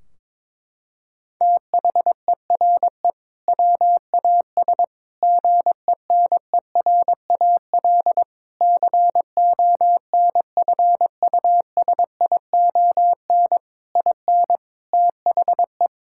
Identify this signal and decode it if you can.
Morse code, heard as a keyed tone, about 22 wpm, 700 Hz. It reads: THERE WAS GENERAL CONFUSION IN THE
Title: Sentences from Top 1600 Words 22wpm